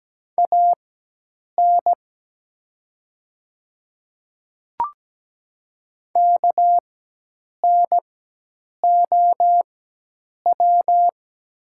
Morse code, heard as a keyed tone, about 17 wpm, spaced as for 8 wpm, 700 Hz. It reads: AN KNOW
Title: 4x Character Spacing - Top 100 Words - 17wpm